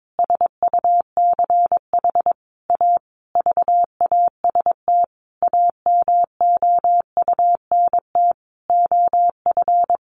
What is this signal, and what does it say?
Morse code, heard as a keyed tone, about 22 wpm, 700 Hz. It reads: SUCH A VAST AMOUNT OF
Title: Sentences from Top 1300 Words 22wpm